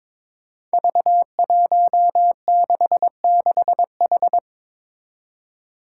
Morse code, heard as a keyed tone, about 22 wpm, 700 Hz. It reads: V166H
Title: Five Challenging Letter-Number Combinations 22wpm